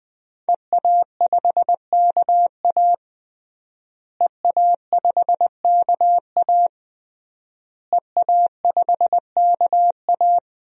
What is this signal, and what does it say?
Morse code, heard as a keyed tone, about 20 wpm, 700 Hz. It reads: EA5KA EA5KA EA5KA
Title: DXCC-100 Call Signs - 5 Characters - Repeated 3x - 20wpm